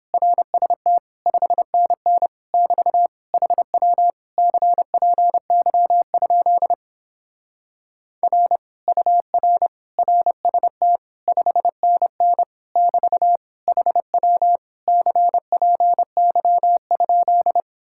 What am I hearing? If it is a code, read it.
Morse code, heard as a keyed tone, about 29 wpm, 700 Hz. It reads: RST 5NN <BT> HW CPY? R UR RST 5NN <BT> HW CPY?